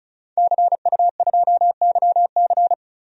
Morse code, heard as a keyed tone, about 36 wpm, 700 Hz. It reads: CU2YC
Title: DXCC-100 Call Signs - 5 Characters - Repeated 2x - 35wpm